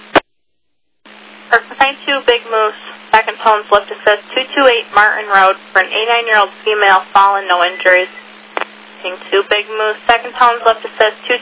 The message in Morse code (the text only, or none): none